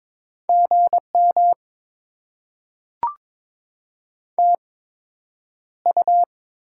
Morse code, heard as a keyed tone, about 22 wpm, spaced as for 9 wpm, 700 Hz. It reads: GM TU
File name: Top 20 QSO Elements - 8x Character Spacing - 22wpm